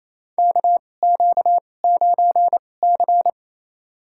Morse code, heard as a keyed tone, about 28 wpm, spaced as for 20 wpm, 700 Hz. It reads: KQ9C
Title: USA Call Signs - 4 Characters - 2x Character Spacing - 28wpm